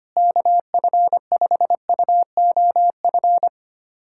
Morse code, heard as a keyed tone, about 25 wpm, 700 Hz. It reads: KF5UOF
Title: USA Callsign DE Callsign - 6 Characters - 25wpm